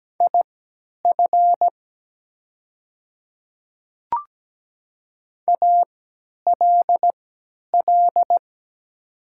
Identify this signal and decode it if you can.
Morse code, heard as a keyed tone, about 17 wpm, spaced as for 10 wpm, 700 Hz. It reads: IF ALL
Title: 3x Character Spacing - Top 100 Words - 17wpm